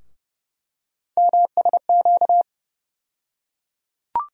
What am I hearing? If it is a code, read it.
Morse code, heard as a keyed tone, about 30 wpm, 700 Hz. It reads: MSQ
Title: USA Call Sign Postfixes - 3 Letters 30wpm